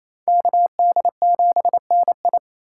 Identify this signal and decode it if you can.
Morse code, heard as a keyed tone, about 34 wpm, 700 Hz. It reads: KD7NI
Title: Speed Racing - USA 2x2 Call Signs 28wpm